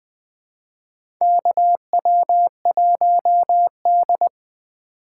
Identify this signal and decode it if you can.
Morse code, heard as a keyed tone, about 21 wpm, 700 Hz. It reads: KW1D
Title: USA Call Signs - 4 Characters - 2x Character Spacing - 20wpm